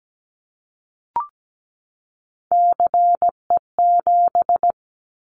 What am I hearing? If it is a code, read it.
Morse code, heard as a keyed tone, about 17 wpm, 700 Hz. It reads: CE7
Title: Callsign Prefixes for Top 100 DXCC - 17wpm